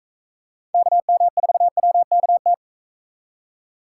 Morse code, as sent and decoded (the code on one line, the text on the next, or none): -.- -- ....- .-- -.- -
KM4WKT